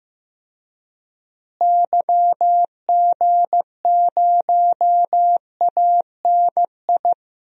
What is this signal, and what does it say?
Morse code, heard as a keyed tone, about 15 wpm, 700 Hz. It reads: YG0ANI
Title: DXCC-100 Call Signs - 6 Characters - Repeated 3x - 15wpm